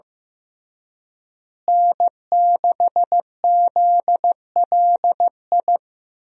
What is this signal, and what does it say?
Morse code, heard as a keyed tone, about 15 wpm, 700 Hz. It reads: N6ZLI
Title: USA Call Signs - 5 Characters - Repeated 2x - 15wpm